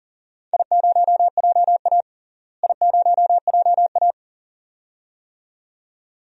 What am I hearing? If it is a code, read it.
Morse code, heard as a keyed tone, about 40 wpm, 700 Hz. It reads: I0JA I0JA
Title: DXCC-100 Call Signs - 4 Characters - Repeated 2x - 40wpm